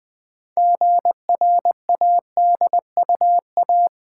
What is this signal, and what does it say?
Morse code, heard as a keyed tone, about 20 wpm, 700 Hz. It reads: GRADUA
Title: Build A Word - 3 Letters At A Time - 9-Letter Words - 20wpm